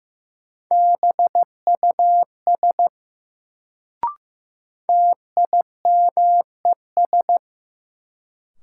Morse code, heard as a keyed tone, about 15 wpm, 700 Hz. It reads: BUS TIMES
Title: Sentences from Top 1000 Words - Review Part 6 - New Word Drill - 15wpm